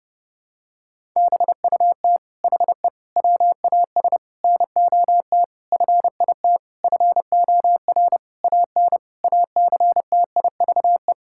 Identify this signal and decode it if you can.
Morse code, heard as a keyed tone, about 32 wpm, 700 Hz. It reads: BUT HE WAS NOT FIT FOR AN ACTIVE
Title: Sentences from Top 1000 Words - Review Part 6 - 30wpm